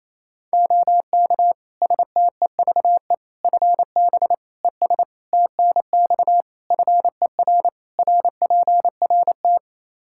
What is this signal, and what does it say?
Morse code, heard as a keyed tone, about 28 wpm, 700 Hz. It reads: OK STEVE FB ES TNX FER RPRT
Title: Practice QSO - Exchange 2 - Info - 28wpm